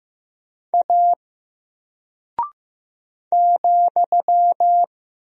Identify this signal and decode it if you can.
Morse code, heard as a keyed tone, about 15 wpm, 700 Hz. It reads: A ,
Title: Single Character (Letter:Number:Punctuation:Prosign) Alt 15wpm